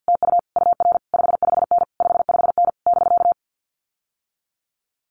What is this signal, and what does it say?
Morse code, heard as a keyed tone, about 50 wpm, 700 Hz. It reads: TU UR 55N 55N <BK>
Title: Common QSO Phrases Encore 50wpm